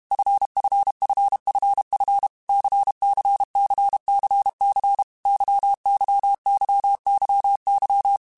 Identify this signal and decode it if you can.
Morse code, heard as a keyed tone, about 32 wpm, 800 Hz. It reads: FFFFF CCCCC YYYYY
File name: Groups@30 / Groups@30 mp3s / LFCYQ@30